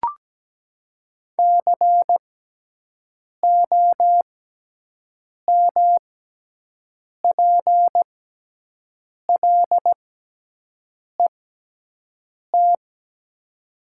Morse code, heard as a keyed tone, about 17 wpm, spaced as for 6 wpm, 700 Hz. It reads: COMPLET
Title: 6x Character Spacing - Top 500 Words - 17wpm